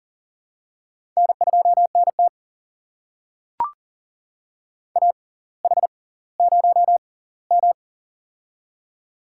Morse code, heard as a keyed tone, about 39 wpm, spaced as for 20 wpm, 690 Hz. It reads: N2NT AH0M
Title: USA Call Signs - 4 Characters - 6x Character Spacing - 40wpm